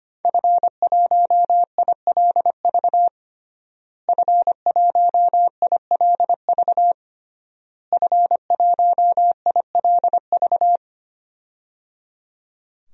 Morse code, heard as a keyed tone, about 25 wpm, 700 Hz. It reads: F1ILV F1ILV F1ILV
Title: DXCC-100 Call Signs - 5 Characters - Repeated 3x - 25wpm